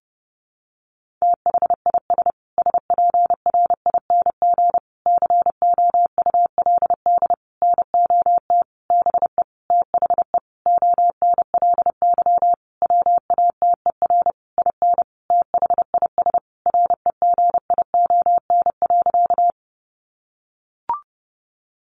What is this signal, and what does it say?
Morse code, heard as a keyed tone, about 30 wpm, 700 Hz. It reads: THIS SPRING COULD NOT BE THE ONLY WATER IN THIS REGION.
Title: Sentences from Top 1500 Words 30wpm